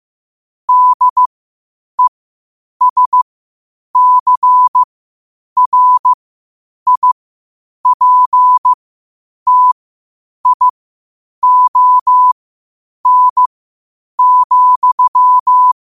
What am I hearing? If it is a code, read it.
Morse code, heard as a keyed tone, about 15 wpm, spaced as for 8 wpm, 1000 Hz. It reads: DESCRIPTION,
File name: qotdfiles / 8wpm